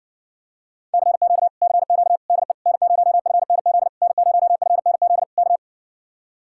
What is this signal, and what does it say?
Morse code, heard as a keyed tone, about 70 wpm, 700 Hz. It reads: CQ CQ DE N0FNZ N0FNZ K